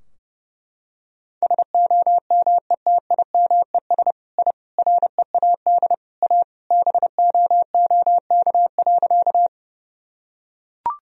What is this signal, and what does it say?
Morse code, heard as a keyed tone, about 31 wpm, 700 Hz. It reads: SOMETIMES I READ A BOOK.
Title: Sentences from Top 1000 Words - Review Part 3 - 30wpm